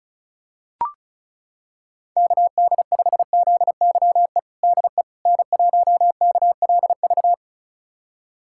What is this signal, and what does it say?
Morse code, heard as a keyed tone, about 36 wpm, 700 Hz. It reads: KD5ZYE DE N1KLV
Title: USA Callsign DE Callsign 35wpm